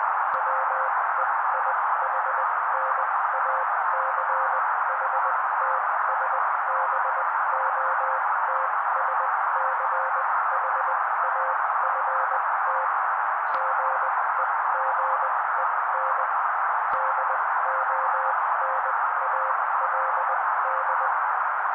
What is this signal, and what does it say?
Morse code, heard as a keyed tone, about 19 wpm, spaced as for 16 wpm, 530 Hz. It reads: WEIHNACHTSBOTSCHAFT GEGEN DONALD
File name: daily / 16wpm noise